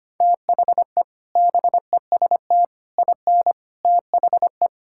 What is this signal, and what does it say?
Morse code, heard as a keyed tone, about 25 wpm, 700 Hz. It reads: THE BEST IN THE
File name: Sentences with 3-Word Phrases - 25wpm